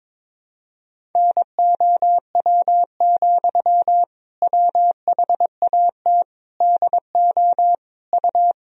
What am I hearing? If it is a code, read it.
Morse code, heard as a keyed tone, about 22 wpm, 700 Hz. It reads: NOW, WHAT DO U
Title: Sentences from Top 200 Words and Abbreviations 22wpm